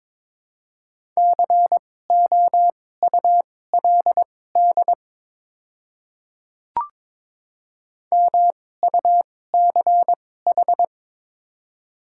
Morse code, heard as a keyed tone, about 22 wpm, spaced as for 16 wpm, 700 Hz. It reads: COULD MUCH
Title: 2x Character Spacing - Top 300 Words - 22wpm